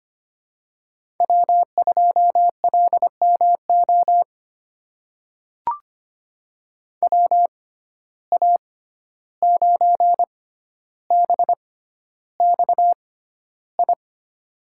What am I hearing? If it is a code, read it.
Morse code, heard as a keyed tone, about 25 wpm, spaced as for 12 wpm, 700 Hz. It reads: W2LMO WA9BXI
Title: USA Call Signs - 6x Character Spacing - 25wpm